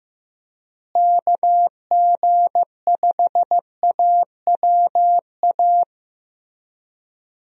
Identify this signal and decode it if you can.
Morse code, heard as a keyed tone, about 15 wpm, 700 Hz. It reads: KG5AWA
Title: USA Call Signs - 6 Characters - Repeated 2x - 15wpm